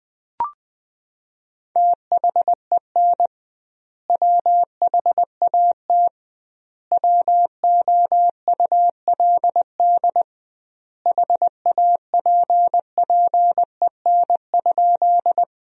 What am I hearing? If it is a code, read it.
Morse code, heard as a keyed tone, about 20 wpm, 700 Hz. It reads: THEN WHAT WOULD HAPPEN?
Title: 2x Word-Spacing - Sentences from Top 400 Words - 20wpm